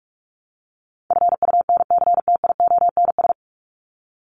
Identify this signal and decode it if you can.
Morse code, heard as a keyed tone, about 44 wpm, 700 Hz. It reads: FUNCTIONS
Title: ES:S Suffix - Top 100 Words with Base Words 45wpm